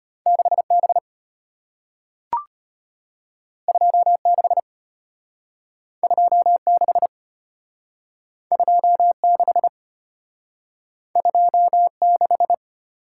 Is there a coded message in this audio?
Yes: Morse code from 0.3 to 12.5 s, heard as a keyed tone, about 34 wpm, 700 Hz. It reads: 6B 26 26 26 26